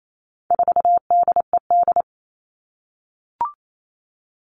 Morse code, heard as a keyed tone, about 28 wpm, 700 Hz. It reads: 4DED
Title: All Cut Numbers - 4-Digit 28wpm